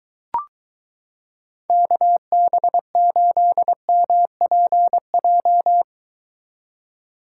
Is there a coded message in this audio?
Yes: Morse code from 1.7 to 5.8 s, heard as a keyed tone, about 22 wpm, 700 Hz. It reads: KB8MPJ